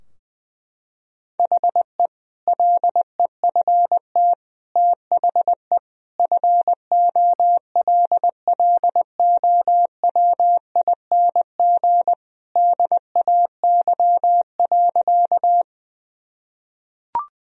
Morse code, heard as a keyed tone, about 20 wpm, 700 Hz. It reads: HE LEFT THE FOLLOWING DAY.
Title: Sentences from Top 1000 Words - Review Part 4 - 20wpm